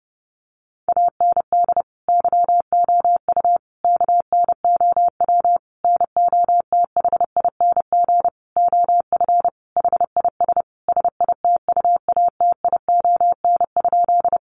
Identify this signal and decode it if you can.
Morse code, heard as a keyed tone, about 30 wpm, 700 Hz. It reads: AND YOU KNOW NOTHING OF HIS SITUATION?
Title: Sentences from Top 1200 Words 30wpm